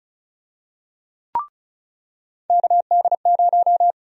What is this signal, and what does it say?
Morse code, heard as a keyed tone, about 34 wpm, 700 Hz. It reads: KD0
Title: USA Call Sign Prefixes - 35wpm